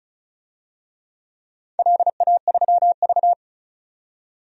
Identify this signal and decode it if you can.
Morse code, heard as a keyed tone, about 36 wpm, 700 Hz. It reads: LA3V
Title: DXCC-100 Call Signs - 4 Characters - 35wpm